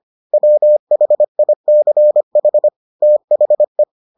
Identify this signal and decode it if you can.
Morse code, heard as a keyed tone, about 25 wpm, 600 Hz. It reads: WHICH THE